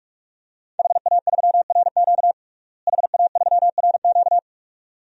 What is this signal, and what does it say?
Morse code, heard as a keyed tone, about 47 wpm, 700 Hz. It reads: HA3RQ HA3RQ
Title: DXCC-100 Call Signs - 5 Characters - Repeated 2x - 45wpm